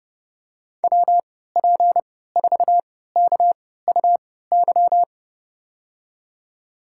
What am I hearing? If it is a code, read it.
Morse code, heard as a keyed tone, about 30 wpm, spaced as for 17 wpm, 700 Hz. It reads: WP4KUY
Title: USA Call Signs - 6 Characters - 3x Character Spacing - 30wpm